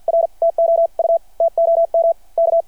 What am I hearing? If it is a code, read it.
Morse code, heard as a keyed tone, about 53 wpm, spaced as for 45 wpm, 690 Hz. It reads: R TOU TOM K
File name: archived / CRSnet